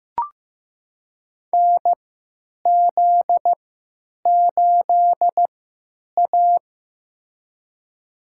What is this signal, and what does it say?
Morse code, heard as a keyed tone, about 15 wpm, spaced as for 9 wpm, 700 Hz. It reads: NZ8A
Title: USA Call Signs - 4 Characters - 3x Character Spacing - 15wpm